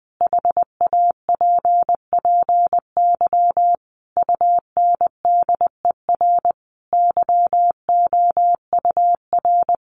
Morse code, heard as a keyed tone, about 20 wpm, 700 Hz. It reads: HAPPY UNDER YOUR
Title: Sentences from Top 1500 Words 20wpm